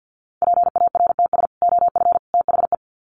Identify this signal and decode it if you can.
Morse code, heard as a keyed tone, about 50 wpm, 700 Hz. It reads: PARTS OF THE
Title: Sets of 3 Words 50wpm